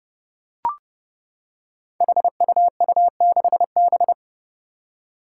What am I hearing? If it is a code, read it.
Morse code, heard as a keyed tone, about 29 wpm, 700 Hz. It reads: HUU6B
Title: Code Groups - 5 Letter:Numbers - 30wpm